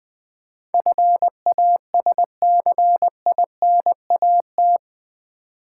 Morse code, heard as a keyed tone, about 20 wpm, 700 Hz. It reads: FASCINAT
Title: Build A Word - 4 Letters At A Time - 11-Letter Words - 20wpm